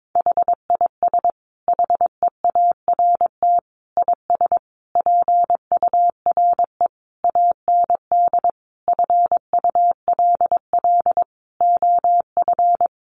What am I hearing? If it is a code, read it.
Morse code, heard as a keyed tone, about 22 wpm, 700 Hz. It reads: HIS HEART IS PURE AND FULL OF
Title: Sentences from Top 1500 Words 22wpm